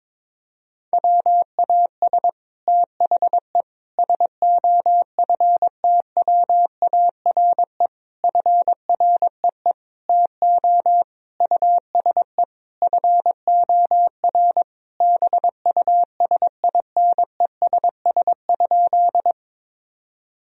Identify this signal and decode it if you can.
Morse code, heard as a keyed tone, about 22 wpm, 700 Hz. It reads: WAS THE SOFTWARE FREE TO USE FOR BUSINESS?
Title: Sentences from Top 1000 Words - Review Part 3 - 22wpm